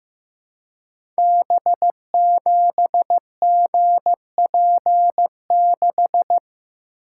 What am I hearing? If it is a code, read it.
Morse code, heard as a keyed tone, about 15 wpm, 700 Hz. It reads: B7GP6
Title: Code Groups - 5 Letter:Numbers - 15wpm